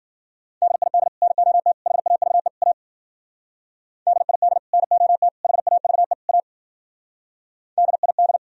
Morse code, heard as a keyed tone, about 52 wpm, 700 Hz. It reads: DID NOT HAVE A DID NOT HAVE A DID